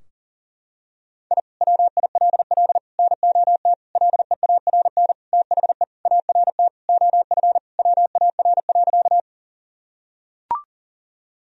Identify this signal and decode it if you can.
Morse code, heard as a keyed tone, about 40 wpm, 700 Hz. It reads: I WILL NOT LEARN THE ART OF WAR.
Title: Sentences from Top 1000 Words - Review Part 3 - 40wpm